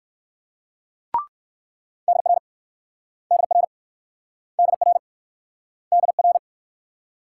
Morse code, heard as a keyed tone, about 56 wpm, 700 Hz. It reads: DR DR DR DR
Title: Speed-Racing Multi-letter Phonemes - 45wpm